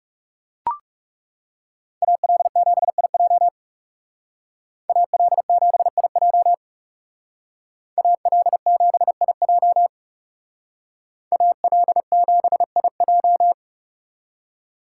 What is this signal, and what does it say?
Morse code, heard as a keyed tone, about 36 wpm, 700 Hz. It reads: AL7IJ AL7IJ AL7IJ AL7IJ
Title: Speed Racing - USA Call Signs 30wpm